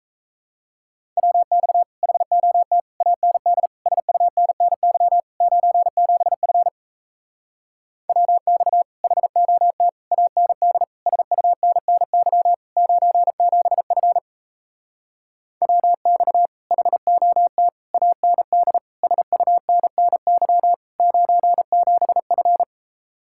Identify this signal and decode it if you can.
Morse code, heard as a keyed tone, about 37 wpm, 700 Hz. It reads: WX HOT AND SUNNY 97F WX HOT AND SUNNY 97F WX HOT AND SUNNY 97F